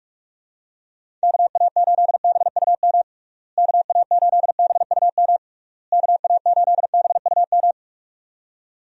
Morse code, heard as a keyed tone, about 46 wpm, 700 Hz. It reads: KA8BUM KA8BUM KA8BUM
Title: USA Call Signs - 6 Characters - Repeated 3x - 45wpm